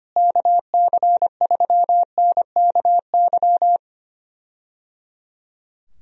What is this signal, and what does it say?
Morse code, heard as a keyed tone, about 25 wpm, 700 Hz. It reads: KC3NKY
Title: USA Call Signs - 6 Characters - Repeated 3x - 25wpm